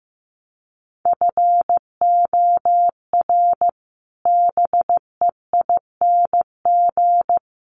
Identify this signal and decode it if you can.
Morse code, heard as a keyed tone, about 15 wpm, 700 Hz. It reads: FOR BEING
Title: Sets of 5 Words Encore2 15wpm